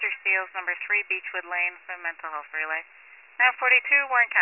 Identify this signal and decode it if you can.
Emergency Alert System (SAME) tones, heard as harsh buzzing bursts, with speech. It is heard from 0.6 to 3.2 s.